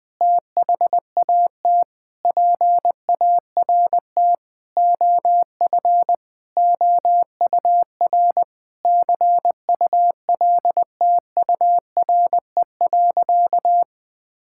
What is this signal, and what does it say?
Morse code, heard as a keyed tone, about 21 wpm, 700 Hz. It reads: THAT PART OF OUR CULTURE.
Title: Sentences from Top 1000 Words - Review Part 6 - 20wpm